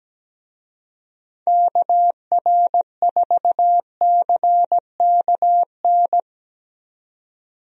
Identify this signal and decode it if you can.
Morse code, heard as a keyed tone, about 17 wpm, 700 Hz. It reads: KR4CKN
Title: USA Call Signs - 6 Characters - 17wpm